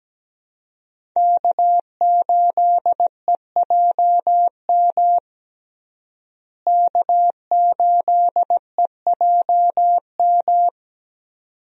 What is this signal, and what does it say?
Morse code, heard as a keyed tone, about 17 wpm, 700 Hz. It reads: K8EJM K8EJM
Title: USA Call Signs - 5 Characters - Repeated 2x - 17wpm